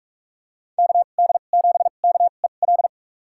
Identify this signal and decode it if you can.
Morse code, heard as a keyed tone, about 47 wpm, spaced as for 33 wpm, 700 Hz. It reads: KD7KEL
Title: USA Call Signs - 6 Characters - 2x Character Spacing - 45wpm